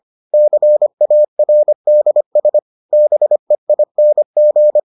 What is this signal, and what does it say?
Morse code, heard as a keyed tone, about 25 wpm, 600 Hz. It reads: CARDS BEING